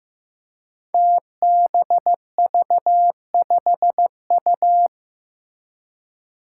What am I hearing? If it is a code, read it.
Morse code, heard as a keyed tone, about 15 wpm, 700 Hz. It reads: TBV5U
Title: Five Challenging Letter-Number Combinations 15wpm